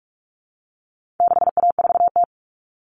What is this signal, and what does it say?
Morse code, heard as a keyed tone, about 44 wpm, 700 Hz. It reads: 6A4T
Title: All Cut Numbers - 4-Digit 45wpm